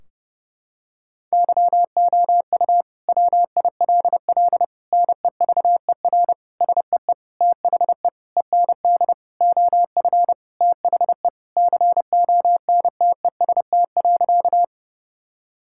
Morse code, heard as a keyed tone, about 30 wpm, 700 Hz. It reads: YOU WILL NEVER SEE THE END OF THE CONTEST.
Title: Sentences from Top 1600 Words 30wpm